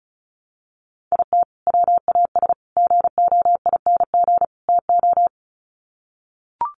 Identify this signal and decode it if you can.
Morse code, heard as a keyed tone, about 34 wpm, 700 Hz. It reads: IT WAS GOING TO
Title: Sets of 4 Words 35wpm